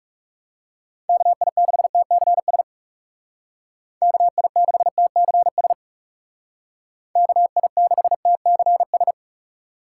Morse code, heard as a keyed tone, about 37 wpm, 700 Hz. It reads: KI6TCS KI6TCS KI6TCS
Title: Speed Racing - USA 2x3 Call Signs 30wpm